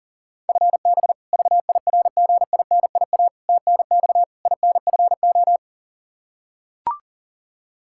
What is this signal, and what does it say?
Morse code, heard as a keyed tone, about 40 wpm, 700 Hz. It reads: FB VIRGINIA TNX INFO
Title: K1USN SST QSO Practice 40wpm